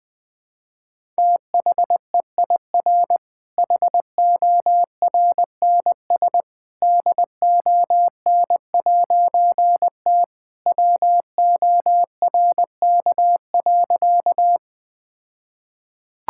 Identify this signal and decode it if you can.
Morse code, heard as a keyed tone, about 20 wpm, 700 Hz. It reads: THEIR HORNS DON'T WORK.
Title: All-Age Appropriate Jokes - 20wpm